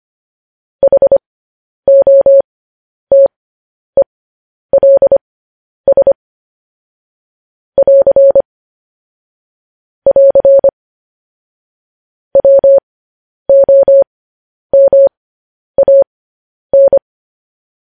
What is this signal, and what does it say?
Morse code, heard as a keyed tone, about 25 wpm, spaced as for 10 wpm, 550 Hz. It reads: HOTELS <AR> <AR> WOMAN
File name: podcast / News headlines-1x-25x10